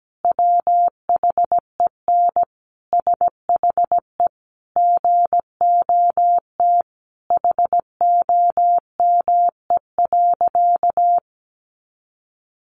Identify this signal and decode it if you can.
Morse code, heard as a keyed tone, about 17 wpm, 700 Hz. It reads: WHEN SHE GOT HOME.
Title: Sentences from Top 1400 Words 17wpm